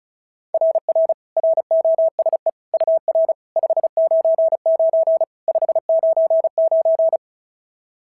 Morse code, heard as a keyed tone, about 33 wpm, 690 Hz. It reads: RR ROSE UR 599 599
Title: SOTA QSO Practice 35wpm